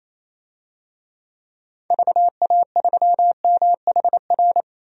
Morse code, heard as a keyed tone, about 28 wpm, 700 Hz. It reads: VA3MHR
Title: DXCC-100 Call Signs - 6 Characters - 28wpm